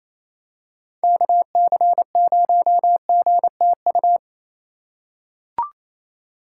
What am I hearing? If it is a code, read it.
Morse code, heard as a keyed tone, about 28 wpm, 700 Hz. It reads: KC0GTU
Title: USA Call Signs - 6 Characters - 2x Character Spacing - 28wpm